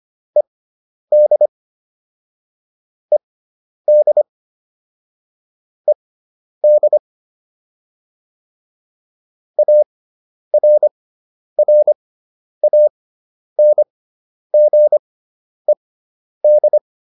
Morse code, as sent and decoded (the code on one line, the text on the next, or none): . -.. / . -.. / . -.. / .- .-. .-. .- -. --. -..
ED ED ED ARRANGED